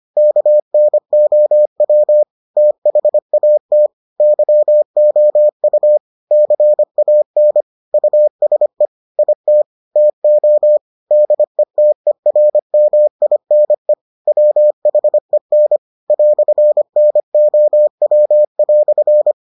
Morse code, heard as a keyed tone, about 25 wpm, 600 Hz. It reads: KNOW THAT YOU CAN USE IT TO DETERMINE WHEN "NOW"